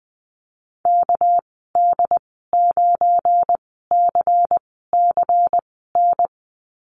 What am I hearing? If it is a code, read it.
Morse code, heard as a keyed tone, about 20 wpm, spaced as for 15 wpm, 700 Hz. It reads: KD9CCN
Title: 200 Lesson 32g - Letter k - Call Signs 20wpm